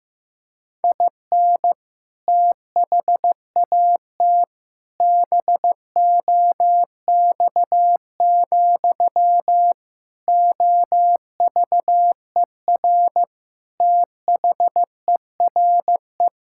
Morse code, heard as a keyed tone, about 15 wpm, 700 Hz. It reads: IN THAT BOX, OVER THERE
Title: Sentences from Top 1600 Words 15wpm